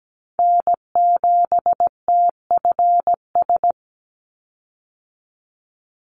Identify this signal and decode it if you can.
Morse code, heard as a keyed tone, about 17 wpm, 700 Hz. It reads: N7TFS
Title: USA Call Signs 17wpm